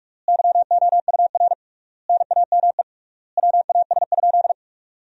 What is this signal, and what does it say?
Morse code, heard as a keyed tone, about 41 wpm, 700 Hz. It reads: YOUR NAME WAS?
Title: 3x Word-Spacing - Sentences from Top 200 Words - 45wpm